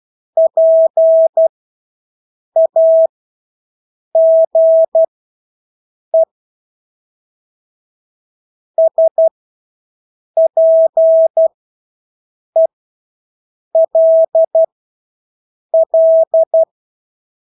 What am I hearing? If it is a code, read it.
Morse code, heard as a keyed tone, about 12 wpm, spaced as for 6 wpm, 650 Hz. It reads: PAGE SPELL
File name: Engelske ord 12-6 wpm